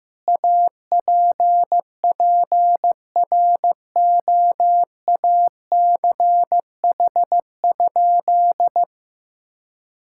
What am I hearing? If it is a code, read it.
Morse code, heard as a keyed tone, about 15 wpm, 700 Hz. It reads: APPROACH?
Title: Sentences from Top 1000 Words - Review Part 6 - 15wpm